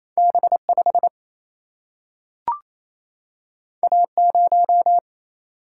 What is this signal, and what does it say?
Morse code, heard as a keyed tone, about 28 wpm, 700 Hz. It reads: B5 A0
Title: Single-Letter Single-Number 28wpm